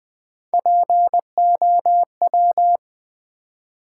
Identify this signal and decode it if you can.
Morse code, heard as a keyed tone, about 20 wpm, 700 Hz. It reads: POW
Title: Build A Word - 1 Letter At A Time - 3 to 5-Letter Words - 20wpm